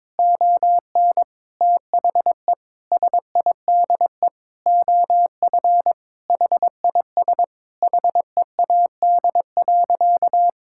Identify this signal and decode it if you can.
Morse code, heard as a keyed tone, about 22 wpm, 700 Hz. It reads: ON THE SIDE OF HIS HEAD.
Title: Sentences from Top 1600 Words 22wpm